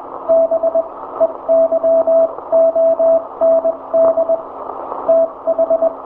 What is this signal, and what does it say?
Morse code, heard as a keyed tone, about 21 wpm, spaced as for 17 wpm, 660 Hz, with radio noise. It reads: BEYOND TH
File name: hamradio / last 500khz cw